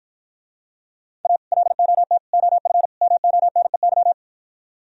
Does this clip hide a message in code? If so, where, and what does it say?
Yes: Morse code from 1.2 to 4.1 s, heard as a keyed tone, about 52 wpm, 700 Hz. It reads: A LOT OF MONEY